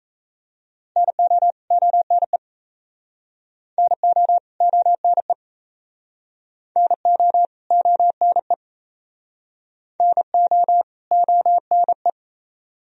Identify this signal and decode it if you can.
Morse code, heard as a keyed tone, about 38 wpm, 700 Hz. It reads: NO ONE NO ONE NO ONE NO ONE